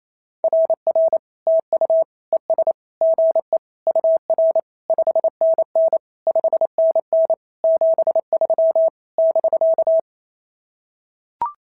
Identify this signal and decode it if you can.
Morse code, heard as a keyed tone, about 28 wpm, 650 Hz. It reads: RR TU ES GE UR 5NN 5NN 73 <BK>
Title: SOTA QSO Practice 28wpm